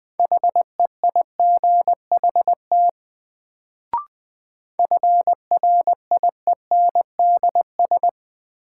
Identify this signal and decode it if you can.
Morse code, heard as a keyed tone, about 20 wpm, 700 Hz. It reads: HEIGHT FRIENDS
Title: Sentences from Top 1600 Words - New Word Drill Alt - 20wpm